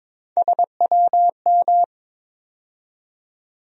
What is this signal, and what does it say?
Morse code, heard as a keyed tone, about 22 wpm, 700 Hz. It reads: SWM
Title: USA Call Sign Postfixes - 3 Letters 22wpm